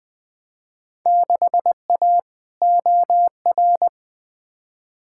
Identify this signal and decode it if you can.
Morse code, heard as a keyed tone, about 20 wpm, 700 Hz. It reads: 6A OR
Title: Field Day Exchanges - 20wpm